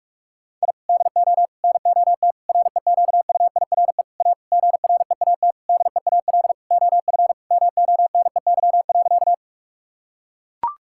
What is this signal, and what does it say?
Morse code, heard as a keyed tone, about 42 wpm, 700 Hz. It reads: I DO NOT REQUIRE A GREAT DEAL OF MONEY.
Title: Sentences from Top 1000 Words - Review Part 6 - 45wpm